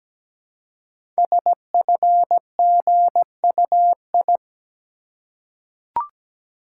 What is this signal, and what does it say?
Morse code, heard as a keyed tone, about 17 wpm, 700 Hz. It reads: SFGUI
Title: Code Groups - 5 Letter:Numbers - 17wpm